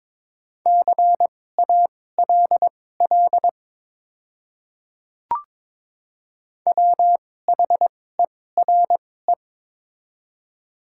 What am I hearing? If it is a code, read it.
Morse code, heard as a keyed tone, about 22 wpm, spaced as for 16 wpm, 700 Hz. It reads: CALL WHERE
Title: 2x Character Spacing - Top 200 Words - 22wpm